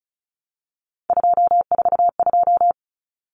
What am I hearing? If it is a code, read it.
Morse code, heard as a keyed tone, about 34 wpm, 700 Hz. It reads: 242
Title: Numbers - 3-Digit 35wpm